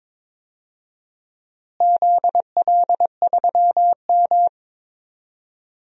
Morse code, heard as a keyed tone, about 21 wpm, 700 Hz. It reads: ZL3M
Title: DXCC-100 Call Signs - 4 Characters - Repeated 2x - 22wpm